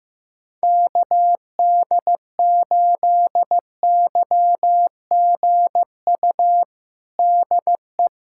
Morse code, heard as a keyed tone, about 15 wpm, 700 Hz. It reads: KD8YGU DE
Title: USA Callsign DE Callsign - 6 Characters - 15wpm